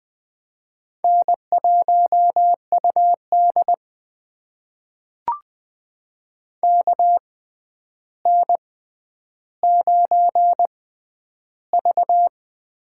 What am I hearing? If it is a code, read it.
Morse code, heard as a keyed tone, about 20 wpm, spaced as for 10 wpm, 700 Hz. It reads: N1UD KN9V
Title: USA Call Signs - 4 Characters - 6x Character Spacing - 20wpm